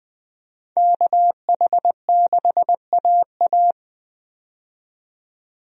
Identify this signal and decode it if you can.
Morse code, heard as a keyed tone, about 20 wpm, 700 Hz. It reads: KH6AA
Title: DXCC-100 Call Signs - 5 Characters - 20wpm